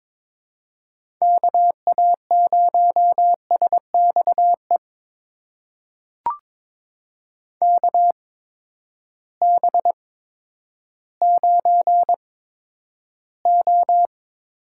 Morse code, heard as a keyed tone, about 22 wpm, spaced as for 11 wpm, 700 Hz. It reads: KA0SXE KB9O
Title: USA Call Signs - 6 Characters - 8x Character Spacing - 22wpm